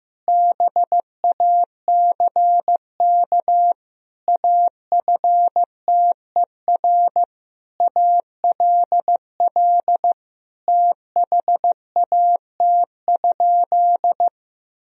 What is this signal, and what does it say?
Morse code, heard as a keyed tone, about 15 wpm, 700 Hz. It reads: BACK AFTER ALL THAT?
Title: Sentences from Top 1000 Words - Review Part 1 - 15wpm